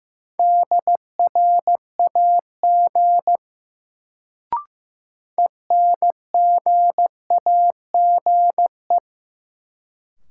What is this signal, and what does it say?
Morse code, heard as a keyed tone, about 15 wpm, 700 Hz. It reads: DRAG ENGAGE
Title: Sentences from Top 1500 Words - New Word Drill - 15wpm